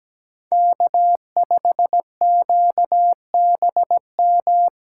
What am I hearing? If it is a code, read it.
Morse code, heard as a keyed tone, about 17 wpm, 700 Hz. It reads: K5QBM
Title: USA Call Signs - 5 Characters - 8x Character Spacing - 17wpm